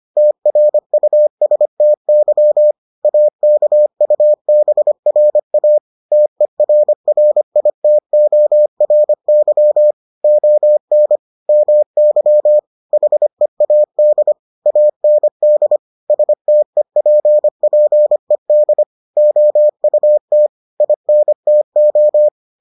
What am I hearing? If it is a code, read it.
Morse code, heard as a keyed tone, about 25 wpm, 600 Hz. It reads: TRUSTY AKUBRA TERRITORY ON MY HEAD AND STEPPED OUT INTO